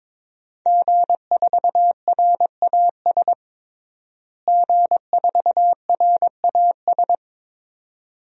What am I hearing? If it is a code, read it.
Morse code, heard as a keyed tone, about 22 wpm, 700 Hz. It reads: G4RAS G4RAS
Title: DXCC-100 Call Signs - 5 Characters - Repeated 2x - 22wpm